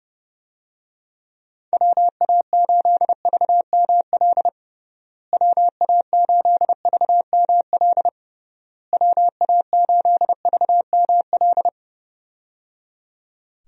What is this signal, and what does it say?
Morse code, heard as a keyed tone, about 30 wpm, 700 Hz. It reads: WA8VML WA8VML WA8VML
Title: USA Call Signs - 6 Characters - Repeated 3x - 30wpm